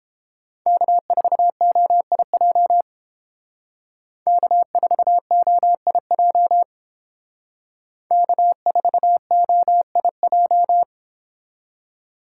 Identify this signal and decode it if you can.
Morse code, heard as a keyed tone, about 30 wpm, 700 Hz. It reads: K4OIJ K4OIJ K4OIJ